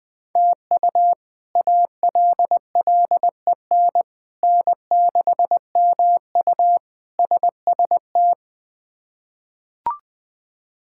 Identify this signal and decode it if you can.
Morse code, heard as a keyed tone, about 20 wpm, 700 Hz. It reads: TU ALLEN N6MU SST
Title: K1USN SST QSO Practice - 20wpm